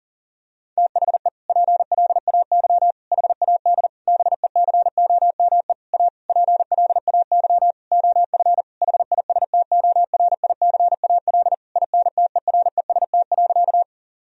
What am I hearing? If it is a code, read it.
Morse code, heard as a keyed tone, about 40 wpm, 700 Hz. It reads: THE PLAY HAD BECOME A PLAY OF HISTORICAL INTEREST.
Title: Sentences from Top 1000 Words - Review Part 4 - 40wpm